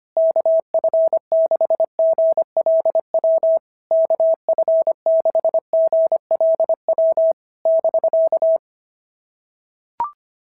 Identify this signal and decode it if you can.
Morse code, heard as a keyed tone, about 23 wpm, 650 Hz. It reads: KF6GLW KF6GLW <BK>
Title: SOTA QSO Practice 25wpm